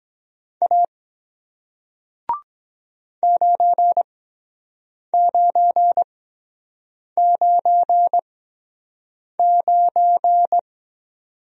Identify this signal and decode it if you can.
Morse code, heard as a keyed tone, about 23 wpm, spaced as for 7 wpm, 700 Hz. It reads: A 9999